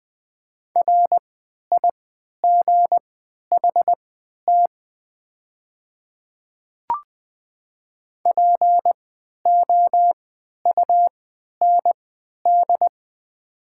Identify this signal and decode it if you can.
Morse code, heard as a keyed tone, about 21 wpm, spaced as for 11 wpm, 700 Hz. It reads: RIGHT POUND
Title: 3x Character Spacing - Top 500 Words - 20wpm